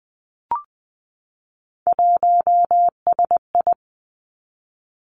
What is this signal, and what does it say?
Morse code, heard as a keyed tone, about 20 wpm, 700 Hz. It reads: 1SI
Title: Three Challenging Letter-Number Combinations 20wpm